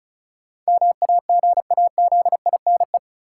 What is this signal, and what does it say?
Morse code, heard as a keyed tone, about 34 wpm, 700 Hz. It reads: MAGAZINE
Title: Sentences from Top 1000 Words - Review Part 7 - New Word Drill - 35wpm